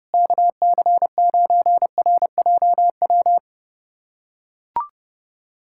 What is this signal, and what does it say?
Morse code, heard as a keyed tone, about 32 wpm, 700 Hz. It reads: KC9RJW
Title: USA Callsign DE Callsign - 6 Characters - 30wpm